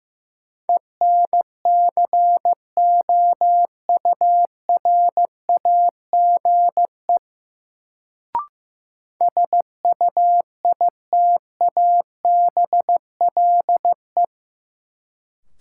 Morse code, heard as a keyed tone, about 15 wpm, 700 Hz. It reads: ENCOURAGE SUITABLE
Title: Sentences from Top 1400 Words - New Word Drill - 15wpm